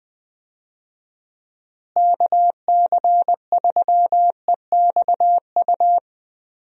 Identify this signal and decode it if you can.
Morse code, heard as a keyed tone, about 20 wpm, 700 Hz. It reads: KC3EXU
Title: USA Call Signs - 6 Characters - Repeated 2x - 20wpm